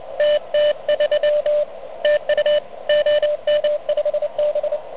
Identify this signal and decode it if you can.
Morse code, heard as a keyed tone, about 30 wpm, 590 Hz, with radio noise. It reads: TT3 TU OM5B
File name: truSDX / MWC29.8